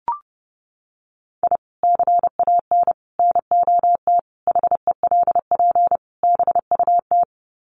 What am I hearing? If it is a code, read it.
Morse code, heard as a keyed tone, about 28 wpm, 700 Hz. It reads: I CAN NOT HELP BUT
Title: Sets of 5 Words Encore2 30wpm